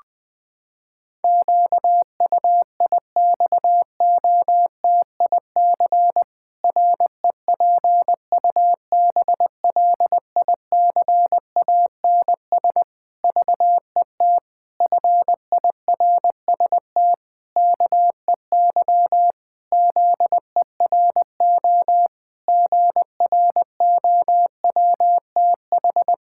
Morse code, heard as a keyed tone, about 21 wpm, 700 Hz. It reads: QUIXOTIC REPUBLICANS VET FIRST KEY ZERO GROWTH
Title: Pangram Challenge v2 - 20wpm